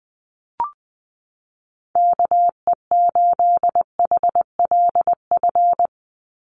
Morse code, heard as a keyed tone, about 20 wpm, 700 Hz. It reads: KE8HLF
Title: USA Call Signs 20wpm